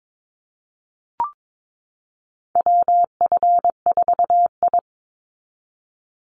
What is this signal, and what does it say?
Morse code, heard as a keyed tone, about 22 wpm, 700 Hz. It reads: WF4I
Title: NA SOTA Chaser Callsigns 22wpm